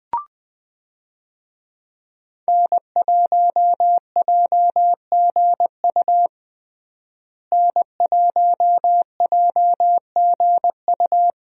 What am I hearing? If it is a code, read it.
Morse code, heard as a keyed tone, about 20 wpm, 700 Hz. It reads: N1JGU N1JGU
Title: USA Call Signs - 5 Characters - Repeated 2x - 20wpm